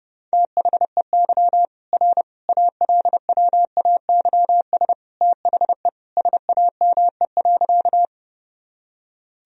Morse code, heard as a keyed tone, about 30 wpm, 700 Hz. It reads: THEY R ALWAYS THE SAME.
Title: Sentences from Top 300 Words and Abbreviations 30wpm